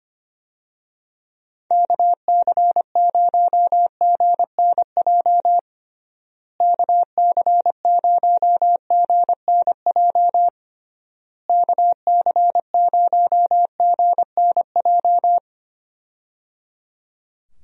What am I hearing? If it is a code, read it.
Morse code, heard as a keyed tone, about 25 wpm, 700 Hz. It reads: KC0GNJ KC0GNJ KC0GNJ
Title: USA Call Signs - 6 Characters - Repeated 3x - 25wpm